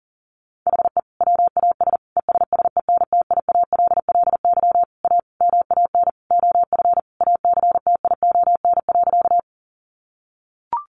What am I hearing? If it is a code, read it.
Morse code, heard as a keyed tone, about 40 wpm, 700 Hz. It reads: HE WAS ESSENTIALLY A MAN OF ACTION.
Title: Sentences from Top 1300 Words 40wpm